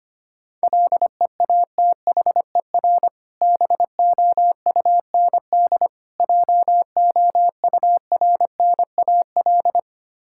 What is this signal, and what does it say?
Morse code, heard as a keyed tone, about 25 wpm, 690 Hz. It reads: LEATHER BOUND JOURNAL
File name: practice / 25wpm / Difficult Words with Hints - 25wpm